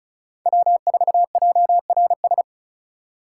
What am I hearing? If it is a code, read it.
Morse code, heard as a keyed tone, about 44 wpm, 700 Hz. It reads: W4JRS
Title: Speed Racing - USA 1x3 Call Signs 35wpm